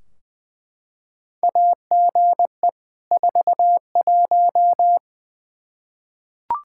Morse code, heard as a keyed tone, about 20 wpm, 700 Hz. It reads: AGE 41
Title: Practice QSO - Exchange 3 - More Info - 20wpm